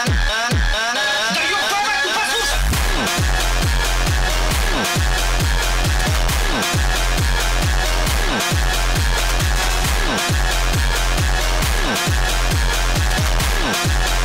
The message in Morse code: none